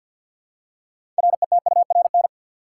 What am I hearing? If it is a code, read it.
Morse code, heard as a keyed tone, about 52 wpm, 700 Hz. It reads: RETURN
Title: Six Letter Words - 50wpm